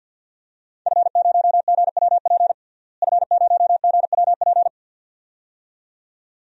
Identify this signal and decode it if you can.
Morse code, heard as a keyed tone, about 49 wpm, 700 Hz. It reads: F0GWP F0GWP
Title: DXCC-100 Call Signs - 5 Characters - Repeated 2x - 50wpm